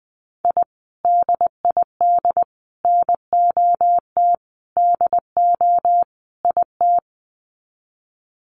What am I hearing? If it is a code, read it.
Morse code, heard as a keyed tone, about 20 wpm, 700 Hz. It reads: I DID NOT DO IT
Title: Sets of 5 Words Encore2 20wpm